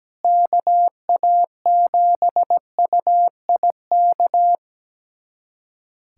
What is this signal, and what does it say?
Morse code, heard as a keyed tone, about 17 wpm, 700 Hz. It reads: KA7UIK
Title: USA Callsign DE Callsign - 6 Characters - 17wpm